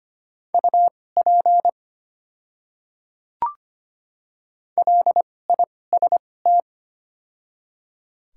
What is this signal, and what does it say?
Morse code, heard as a keyed tone, about 25 wpm, spaced as for 18 wpm, 700 Hz. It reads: UP LIST
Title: 2x Character Spacing - Top 500 Words - 25wpm